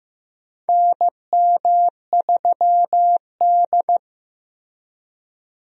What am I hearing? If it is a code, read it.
Morse code, heard as a keyed tone, about 15 wpm, 700 Hz. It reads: NM3D
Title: USA Call Signs - 4 Characters - 15wpm